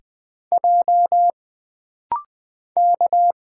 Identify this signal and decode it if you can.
Morse code, heard as a keyed tone, about 20 wpm, 700 Hz. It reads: J K
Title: Single Letter-Number - Rapid-Fire 20wpm